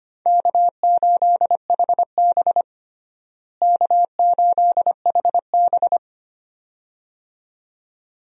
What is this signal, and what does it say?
Morse code, heard as a keyed tone, about 25 wpm, 700 Hz. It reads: K8HB K8HB
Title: USA Call Signs - 4 Characters - Repeated 2x - 25wpm